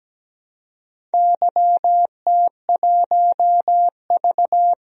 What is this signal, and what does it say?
Morse code, heard as a keyed tone, about 17 wpm, 700 Hz. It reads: YT1V
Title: DXCC-100 Call Signs - 4 Characters - 17wpm